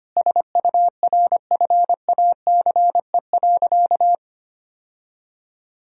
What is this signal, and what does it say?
Morse code, heard as a keyed tone, about 25 wpm, 700 Hz. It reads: SURFACE.
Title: Sentences from Top 1000 Words - Review Part 3 - 25wpm